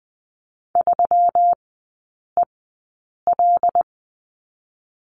Morse code, heard as a keyed tone, about 20 wpm, 700 Hz. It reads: 3 E L
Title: 121 Lesson 20e - Number 3 - Three Characters 20wpm